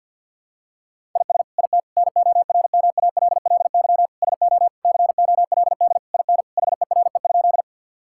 Morse code, heard as a keyed tone, about 52 wpm, 700 Hz. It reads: IS IT NORMALLY SO COLD IN HERE?
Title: Sentences from Top 1000 Words - Review Part 6 - 50wpm